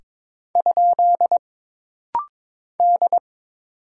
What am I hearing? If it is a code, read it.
Morse code, heard as a keyed tone, about 23 wpm, 700 Hz. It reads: ? D
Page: This rapid-fire timing is geared to push your ICR (Instant Character Recognition) skills.